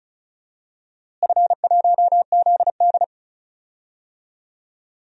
Morse code, heard as a keyed tone, about 34 wpm, 700 Hz. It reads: F1ZD
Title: DXCC-100 Call Signs - 4 Characters - Repeated 2x - 35wpm